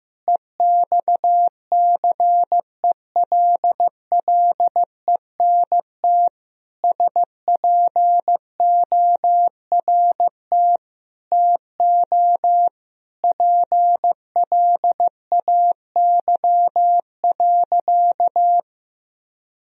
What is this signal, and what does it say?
Morse code, heard as a keyed tone, about 15 wpm, 700 Hz. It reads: EXCELLENT SPORT TO PLAY.
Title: Sentences from Top 1000 Words - Review Part 4 - 15wpm